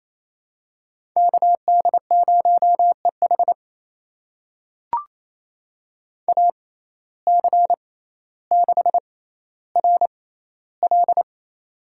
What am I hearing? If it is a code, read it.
Morse code, heard as a keyed tone, about 28 wpm, spaced as for 14 wpm, 700 Hz. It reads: KD0EH AC6RL
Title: USA Call Signs - 5 Characters - 6x Character Spacing - 28wpm